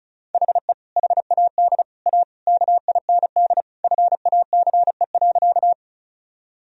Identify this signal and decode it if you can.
Morse code, heard as a keyed tone, about 36 wpm, 700 Hz. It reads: HE HAD A KIND FACE.
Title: Sentences from Top 1000 Words - Review Part 2 - 35wpm